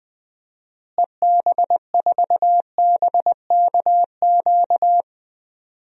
Morse code, heard as a keyed tone, about 20 wpm, 700 Hz. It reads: EB4BKQ
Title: DXCC-100 Call Signs - 6 Characters - Repeated 2x - 20wpm